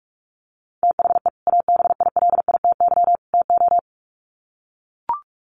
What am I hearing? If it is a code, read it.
Morse code, heard as a keyed tone, about 44 wpm, 700 Hz. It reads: THE ABILITY TO
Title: Sets of 3 Words 45wpm